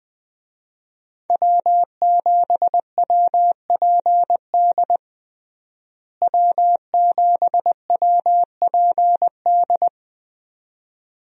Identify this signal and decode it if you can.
Morse code, heard as a keyed tone, about 21 wpm, 700 Hz. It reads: W7WPD W7WPD
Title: USA Call Signs - 5 Characters - Repeated 2x - 20wpm